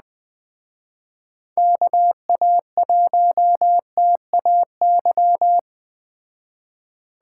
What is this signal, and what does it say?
Morse code, heard as a keyed tone, about 20 wpm, 700 Hz. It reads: KA1TAY